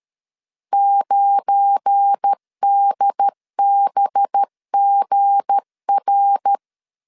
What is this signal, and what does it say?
Morse code, heard as a keyed tone, about 13 wpm, 800 Hz. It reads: 9DBGR
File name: LEZIONI / 7 GF9Z / 7 GF9 12wpm 2319